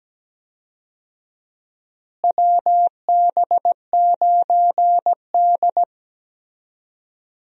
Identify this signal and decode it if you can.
Morse code, heard as a keyed tone, about 17 wpm, 700 Hz. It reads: WB9D
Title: USA Call Signs - 4 Characters - Repeated 2x - 17wpm